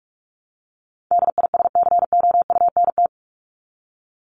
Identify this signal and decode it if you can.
Morse code, heard as a keyed tone, about 44 wpm, 700 Hz. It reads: DISCOUNT
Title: Top 1001-1250 Words 45wpm